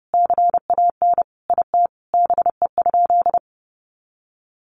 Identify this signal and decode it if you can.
Morse code, heard as a keyed tone, about 28 wpm, 700 Hz. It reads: CAN IT BE?
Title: Sentences from Top 100 Words 30wpm